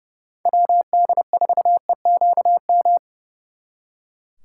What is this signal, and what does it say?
Morse code, heard as a keyed tone, about 36 wpm, 700 Hz. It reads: WD4EQM
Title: Speed Racing - USA 2x3 Call Signs 30wpm